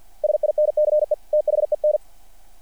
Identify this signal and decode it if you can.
Morse code, heard as a keyed tone, about 48 wpm, 600 Hz. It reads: SINCE THEN
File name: archived / CRSnet